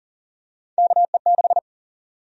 Morse code, heard as a keyed tone, about 39 wpm, 700 Hz. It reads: KE6
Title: USA Call Sign Prefixes - 40wpm